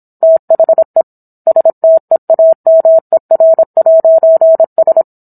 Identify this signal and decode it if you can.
Morse code, heard as a keyed tone, about 27 wpm, 650 Hz. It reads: THE STEAMER'S